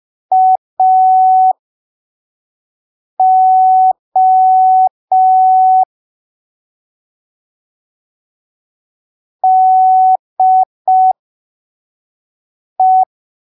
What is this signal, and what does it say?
Morse code, heard as a keyed tone, about 5 wpm, spaced as for 3 wpm, 750 Hz. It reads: AO DE
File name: arquivos / texto / texto 06 05ppm